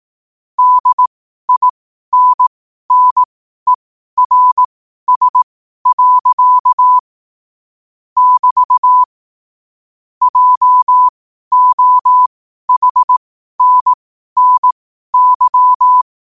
Morse code, heard as a keyed tone, about 18 wpm, spaced as for 12 wpm, 1000 Hz. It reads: DINNERS. <BT> JOHNNY
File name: qotdfiles / 12wpm